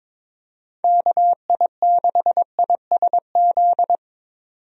Morse code, heard as a keyed tone, about 22 wpm, 700 Hz. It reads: KI6ISZ
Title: USA Call Signs - 6 Characters - 22wpm